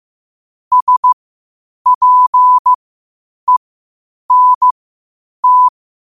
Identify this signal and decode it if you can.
Morse code, heard as a keyed tone, about 15 wpm, spaced as for 8 wpm, 1000 Hz. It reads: SPENT
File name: qotdfiles / 8wpm